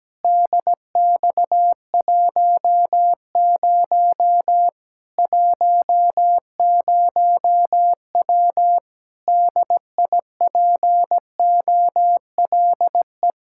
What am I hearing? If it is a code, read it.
Morse code, heard as a keyed tone, about 17 wpm, 700 Hz. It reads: DX10 10W DIPOLE
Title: Practice QSO - Exchange 2 - Info - 17wpm